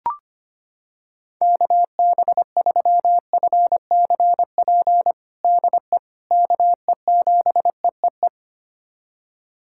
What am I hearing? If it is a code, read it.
Morse code, heard as a keyed tone, about 25 wpm, 700 Hz. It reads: KB3FCP DE KE7EEE
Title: USA Callsign DE Callsign 25wpm